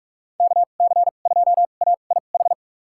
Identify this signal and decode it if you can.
Morse code, heard as a keyed tone, about 42 wpm, spaced as for 31 wpm, 700 Hz. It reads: KC2AIH
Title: USA Call Signs - 6 Characters - 2x Character Spacing - 45wpm